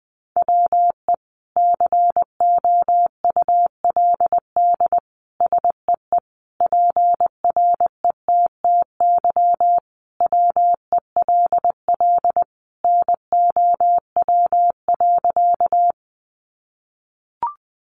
Morse code, heard as a keyed tone, about 20 wpm, 700 Hz. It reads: WE COULD SEE PRETTY WELL NOW.
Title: Sentences from Top 1000 Words 20wpm